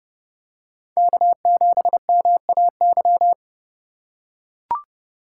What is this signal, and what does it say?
Morse code, heard as a keyed tone, about 30 wpm, 700 Hz. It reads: K7MAY